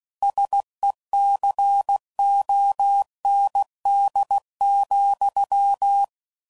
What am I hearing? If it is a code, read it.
Morse code, heard as a keyed tone, about 16 wpm, 800 Hz. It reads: SECOND,
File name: CW Code Groups @15WPM / Technical Subjects / Reactance